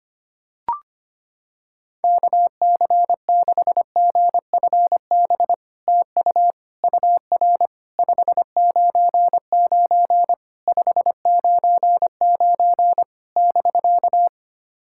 Morse code, heard as a keyed tone, about 25 wpm, 700 Hz. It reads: KC6GFB TU UR 599 599 <BK>
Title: SOTA QSO Practice 25wpm